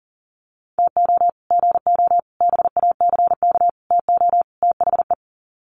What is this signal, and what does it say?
Morse code, heard as a keyed tone, about 40 wpm, 700 Hz. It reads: TO GO BACK TO THE
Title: Sets of 5 Words 40wpm